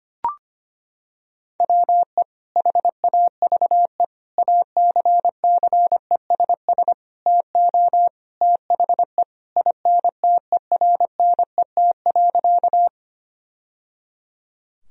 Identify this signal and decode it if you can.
Morse code, heard as a keyed tone, about 25 wpm, 700 Hz. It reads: WE HAVE ACCESS TO THE INTERNET.
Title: Sentences from Top 1000 Words - Review Part 4 - 25wpm